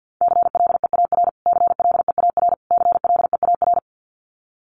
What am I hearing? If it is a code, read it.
Morse code, heard as a keyed tone, about 48 wpm, 700 Hz. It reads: CLEAR CLEAR CLEAR
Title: Top 500 Words Repeated 50wpm